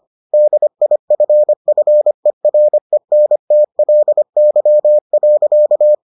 Morse code, heard as a keyed tone, about 25 wpm, 600 Hz. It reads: DIFFERENTLY.